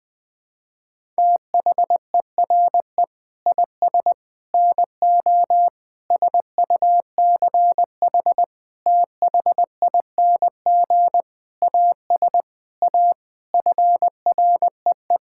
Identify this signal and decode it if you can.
Morse code, heard as a keyed tone, about 20 wpm, 700 Hz. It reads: THERE IS NO SUCH THING AS A FREE
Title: Sentences with 9-Word Phrases - 20wpm